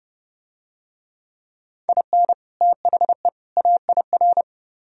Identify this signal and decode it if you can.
Morse code, heard as a keyed tone, about 30 wpm, 700 Hz. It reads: IN THE AIR